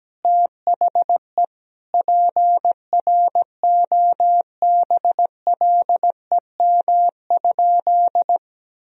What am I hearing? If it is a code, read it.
Morse code, heard as a keyed tone, about 17 wpm, 700 Hz. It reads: THE PROBLEM?
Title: Sentences from Top 1000 Words - Review Part 4 - 17wpm